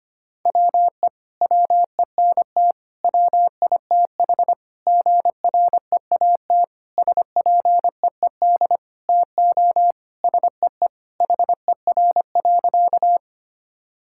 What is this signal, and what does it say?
Morse code, heard as a keyed tone, about 25 wpm, 700 Hz. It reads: WE WENT WITH GREAT SPEED TO SEE HER.
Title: Sentences from Top 1000 Words - Review Part 5 - 25wpm